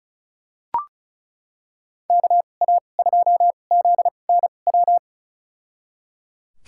Morse code, heard as a keyed tone, about 36 wpm, spaced as for 25 wpm, 700 Hz. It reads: KA2ZNW
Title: USA Call Signs - 6 Characters - 2x Character Spacing - 35wpm